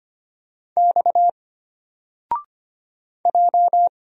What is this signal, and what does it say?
Morse code, heard as a keyed tone, about 25 wpm, 700 Hz. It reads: X J
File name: Single-Letter-Number-25wpm